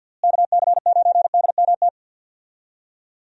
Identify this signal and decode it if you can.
Morse code, heard as a keyed tone, about 51 wpm, 700 Hz. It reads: KC9DMT
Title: USA Call Signs - 6 Characters - Repeated 2x - 50wpm